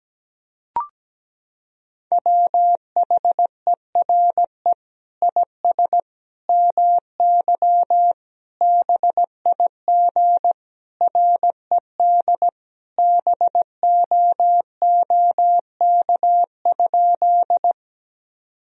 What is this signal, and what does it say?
Morse code, heard as a keyed tone, about 17 wpm, 700 Hz. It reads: WHERE IS MY BIG RED BOOK?
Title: Sentences from Top 1000 Words - Review Part 2 - 17wpm